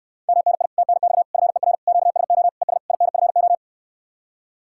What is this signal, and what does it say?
Morse code, heard as a keyed tone, about 67 wpm, 700 Hz. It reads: MNI TNX FER QSO ES INFO